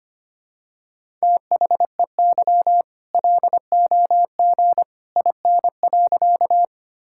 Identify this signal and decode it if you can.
Morse code, heard as a keyed tone, about 25 wpm, 700 Hz. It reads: THEY LOG IN.
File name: practice / 25wpm / All-Age Appropriate Jokes - 25wpm